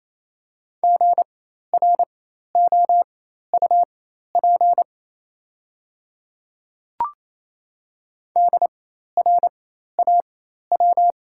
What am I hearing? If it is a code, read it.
Morse code, heard as a keyed tone, about 28 wpm, spaced as for 13 wpm, 700 Hz. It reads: GROUP DRAW
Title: 4x Character Spacing - Top 300 Words - 28wpm